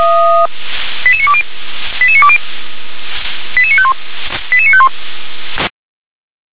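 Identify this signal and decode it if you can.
Five-tone selective call (ZVEI), heard as quick runs of five tones, tones of about 70 ms. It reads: 88920 88920 88951 88951
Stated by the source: Five-tone ZVEI selective call with code 88988, received by SDR.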